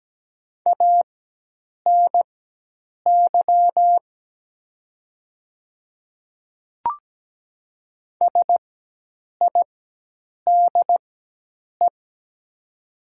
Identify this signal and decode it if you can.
Morse code, heard as a keyed tone, about 17 wpm, spaced as for 8 wpm, 700 Hz. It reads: ANY SIDE
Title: 4x Character Spacing - Top 200 Words - 17wpm